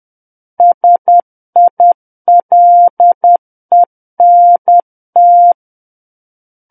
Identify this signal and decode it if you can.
Morse code, heard as a keyed tone, about 10 wpm, 700 Hz. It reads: SILENT
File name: Book One / 01-Chapter 1 - 10 WPM